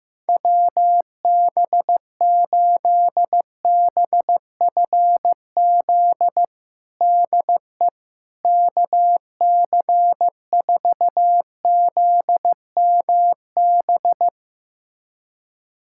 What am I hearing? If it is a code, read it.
Morse code, heard as a keyed tone, about 15 wpm, 700 Hz. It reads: WB8BFZ DE KC4ZMB
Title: USA Callsign DE Callsign - 6 Characters - 15wpm